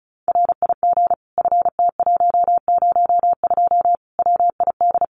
Morse code, heard as a keyed tone, about 34 wpm, 700 Hz. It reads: RIG FT102 WID
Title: Common QSO Phrases 35wpm